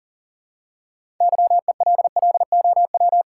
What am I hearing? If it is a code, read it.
Morse code, heard as a keyed tone, about 40 wpm, 700 Hz. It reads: YELLOW
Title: Six Letter Words - 40wpm